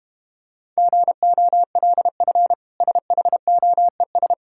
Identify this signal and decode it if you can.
Morse code, heard as a keyed tone, about 32 wpm, 700 Hz. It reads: GOLF SHOES